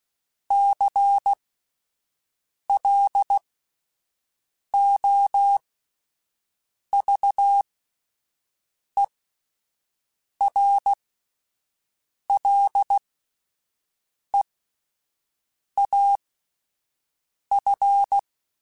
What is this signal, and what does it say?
Morse code, heard as a keyed tone, about 16 wpm, spaced as for 5 wpm, 800 Hz. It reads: CLOVERLEAF
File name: fromSMs / FromSMs mp3s / Cities@5